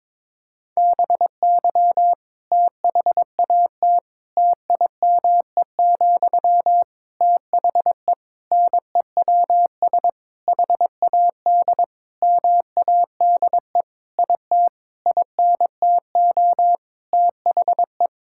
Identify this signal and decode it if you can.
Morse code, heard as a keyed tone, about 22 wpm, 700 Hz. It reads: BY THAT TIME, THE NEWS HAD MADE IT INTO THE
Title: Sentences from Top 1600 Words 22wpm